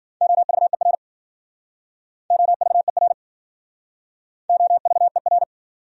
Morse code, heard as a keyed tone, about 52 wpm, 700 Hz. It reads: OVER OVER OVER